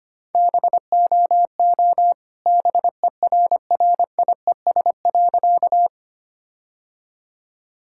Morse code, heard as a keyed tone, about 25 wpm, 700 Hz. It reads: BOO BERRIES.
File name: practice / 25wpm / All-Age Appropriate Jokes - 25wpm